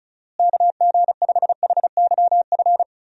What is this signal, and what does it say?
Morse code, heard as a keyed tone, about 34 wpm, 700 Hz. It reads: KG5HYF
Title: USA Call Signs - 6 Characters - Repeated 3x - 35wpm